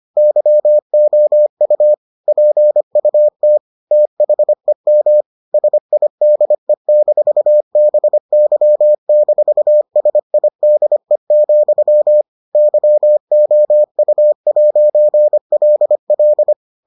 Morse code, heard as a keyed tone, about 25 wpm, 600 Hz. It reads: YOU PUT THEM SIDE-BY-SIDE, YOU'LL